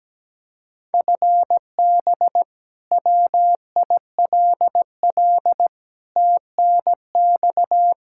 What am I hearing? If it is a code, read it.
Morse code, heard as a keyed tone, about 17 wpm, 700 Hz. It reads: FB WILL TNX
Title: Practice QSO - Exchange 3 - More Info - 17wpm